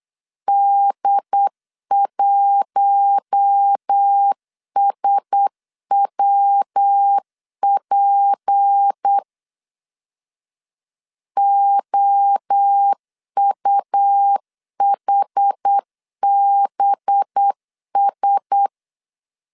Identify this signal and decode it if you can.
Morse code, heard as a keyed tone, about 8 wpm, 800 Hz. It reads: D1SWP OUHBS
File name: CW / DET / 1 6 1500C S B 8wpm 5142